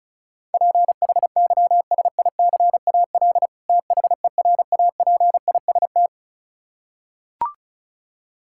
Morse code, heard as a keyed tone, about 34 wpm, 700 Hz. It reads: PHYSICAL THERAPIST
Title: Practice QSO - Exchange 3 - More Info - 35wpm